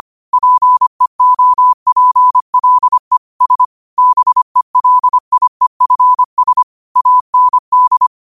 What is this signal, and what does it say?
Morse code, heard as a keyed tone, about 25 wpm, 1000 Hz. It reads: PEOPLE S BELIEFS AND
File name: qotdfiles / 25wpm